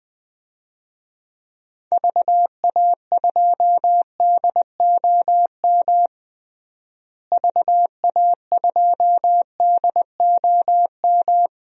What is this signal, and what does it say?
Morse code, heard as a keyed tone, about 20 wpm, 700 Hz. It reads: VA2DOM VA2DOM
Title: DXCC-100 Call Signs - 6 Characters - Repeated 3x - 20wpm